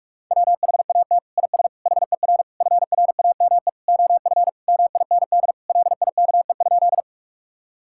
Morse code, heard as a keyed tone, about 46 wpm, 700 Hz. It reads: WHAT IS HER FRAME OF MIND LIKE?
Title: Sentences from Top 1000 Words - Review Part 6 - 45wpm